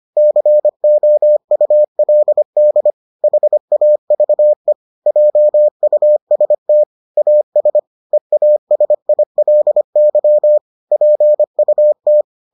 Morse code, heard as a keyed tone, about 25 wpm, 600 Hz. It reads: COULD HAVE JUST AS EASILY PUT